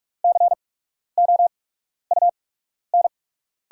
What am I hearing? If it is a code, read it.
Morse code, heard as a keyed tone, about 44 wpm, spaced as for 12 wpm, 700 Hz. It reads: COUN
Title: 8x Character Spacing - Top 300 Words - 45wpm